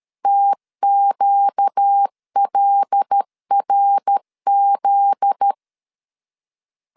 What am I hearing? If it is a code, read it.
Morse code, heard as a keyed tone, about 13 wpm, 800 Hz. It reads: TQLRZ
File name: LEZIONI / 8 YJ45 / 8 Y 12wpm 2228